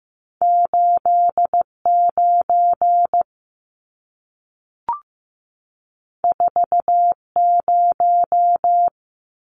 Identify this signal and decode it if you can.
Morse code, heard as a keyed tone, about 15 wpm, 700 Hz. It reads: 89 40
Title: Numbers - 2-Digit 15wpm